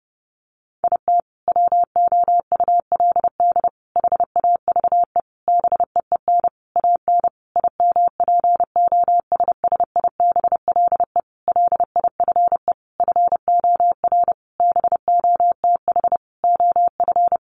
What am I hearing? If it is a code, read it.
Morse code, heard as a keyed tone, about 30 wpm, 700 Hz. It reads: IT WOULD HAVE BEEN AN IMPOSSIBLE LIFE FOR BOTH OF
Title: Sentences from Top 1100 Words 30wpm